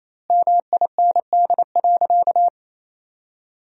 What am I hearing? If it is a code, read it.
Morse code, heard as a keyed tone, about 27 wpm, 700 Hz. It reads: MIND.
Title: 2x Word-Spacing - Sentences from Top 500 Words - 28wpm